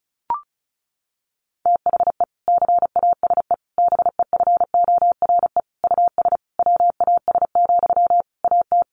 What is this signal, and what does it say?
Morse code, heard as a keyed tone, about 34 wpm, 700 Hz. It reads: THE CASE BEFORE US WAS, AT
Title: Sentences from Top 1400 Words 35wpm